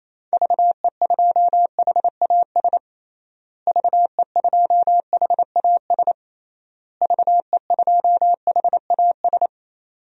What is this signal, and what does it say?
Morse code, heard as a keyed tone, about 28 wpm, 700 Hz. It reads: VE2HAS VE2HAS VE2HAS
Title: DXCC-100 Call Signs - 6 Characters - Repeated 3x - 28wpm